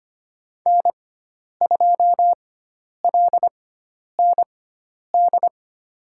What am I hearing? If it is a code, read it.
Morse code, heard as a keyed tone, about 25 wpm, spaced as for 10 wpm, 700 Hz. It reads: N2LND
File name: Lesson 16g - Letter c - Call Signs 25wpm